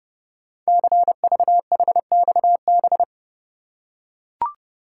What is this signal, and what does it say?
Morse code, heard as a keyed tone, about 30 wpm, 700 Hz. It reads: CVHXB
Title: Code Groups - 5 Letters - 30wpm